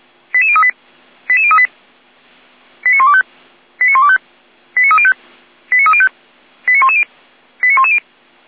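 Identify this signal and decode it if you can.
Five-tone selective call (ZVEI), heard as quick runs of five tones, tones of about 70 ms. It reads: 88038 88038 89125 89125 89305 89305 89110 89110
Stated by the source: ZVEI five-tone selective call received on an SDR.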